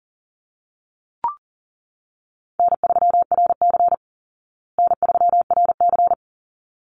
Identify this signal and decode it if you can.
Morse code, heard as a keyed tone, about 37 wpm, 700 Hz. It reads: N3RC N3RC
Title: Call Signs Repeated 3x - 40wpm